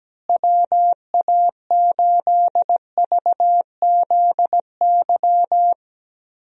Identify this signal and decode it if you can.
Morse code, heard as a keyed tone, about 17 wpm, 700 Hz. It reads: WA8VZY
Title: USA Call Signs - 6 Characters - Repeated 3x - 17wpm